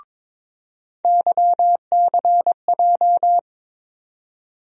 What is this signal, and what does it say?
Morse code, heard as a keyed tone, about 22 wpm, 700 Hz. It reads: YCJ
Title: USA Call Sign Postfixes - 3 Letters 22wpm